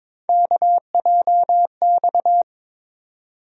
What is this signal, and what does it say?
Morse code, heard as a keyed tone, about 22 wpm, 700 Hz. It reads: KJX
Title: USA Call Sign Postfixes - 3 Letters 22wpm